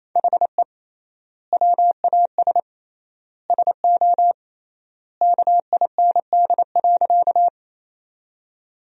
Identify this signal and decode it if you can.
Morse code, heard as a keyed tone, about 28 wpm, 700 Hz. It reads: HE WAS SO KIND.
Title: 3x Word-Spacing - Sentences from Top 200 Words - 28wpm